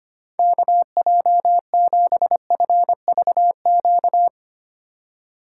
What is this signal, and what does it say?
Morse code, heard as a keyed tone, about 25 wpm, 700 Hz. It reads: KJ7FVQ
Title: USA Call Signs - 4x Character Spacing - 25wpm